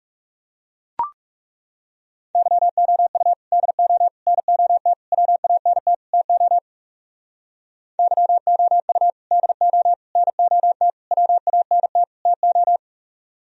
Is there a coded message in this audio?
Yes: Morse code from 2.4 to 12.8 s, heard as a keyed tone, about 39 wpm, 700 Hz. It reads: YOU DO NOT WANT TO YOU DO NOT WANT TO